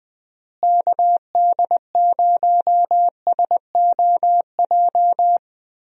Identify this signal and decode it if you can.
Morse code, heard as a keyed tone, about 21 wpm, 700 Hz. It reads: KD0SOJ
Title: USA Call Signs - 6 Characters - Repeated 2x - 20wpm